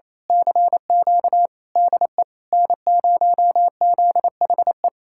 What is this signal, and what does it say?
Morse code, heard as a keyed tone, about 26 wpm, 700 Hz. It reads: CQ DE N0ZHE
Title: Speed-Racing Common QSO Phrases - 28wpm